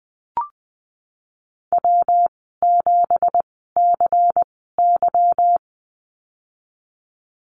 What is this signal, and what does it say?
Morse code, heard as a keyed tone, about 19 wpm, spaced as for 14 wpm, 700 Hz. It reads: W7CY
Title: 173 Lesson 28g - Number 9 - Call Signs 20wpm